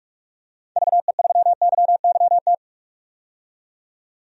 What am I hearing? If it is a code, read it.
Morse code, heard as a keyed tone, about 43 wpm, 700 Hz. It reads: VE3YYT
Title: DXCC-100 Call Signs - 6 Characters - 45wpm